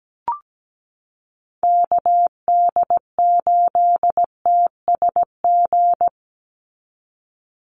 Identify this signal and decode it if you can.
Morse code, heard as a keyed tone, about 17 wpm, 700 Hz. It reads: KD8TSG
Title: USA Call Signs 17wpm